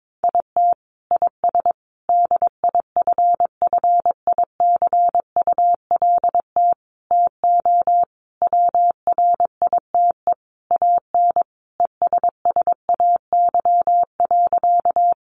Morse code, heard as a keyed tone, about 22 wpm, 700 Hz. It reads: IT IS DIFFICULT TO WRITE AN ESSAY.
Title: Sentences from Top 1500 Words 22wpm